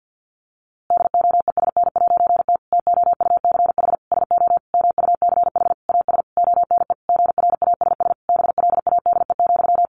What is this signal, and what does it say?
Morse code, heard as a keyed tone, about 49 wpm, 700 Hz. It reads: DOESN'T TOUCH SO MUCH AS ONE GRASS BLADE,
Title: A Sound Of Thunder 50wpm